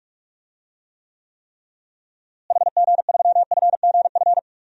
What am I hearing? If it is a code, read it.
Morse code, heard as a keyed tone, about 46 wpm, 700 Hz. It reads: HG3FGF
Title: DXCC-100 Call Signs - 6 Characters - Repeated 3x - 45wpm